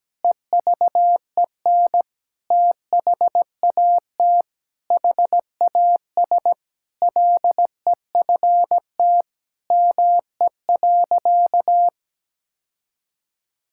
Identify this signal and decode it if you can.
Morse code, heard as a keyed tone, about 17 wpm, 700 Hz. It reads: EVEN THAT HAS LEFT ME.
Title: Sentences from Top 1600 Words 17wpm